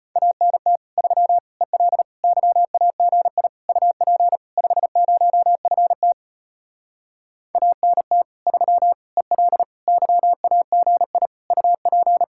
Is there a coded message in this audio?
Yes: Morse code from 0.2 to 12.2 s, heard as a keyed tone, about 33 wpm, 700 Hz. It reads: ANT 3 EL YAGI UP 50FT ANT 3 EL YAGI UP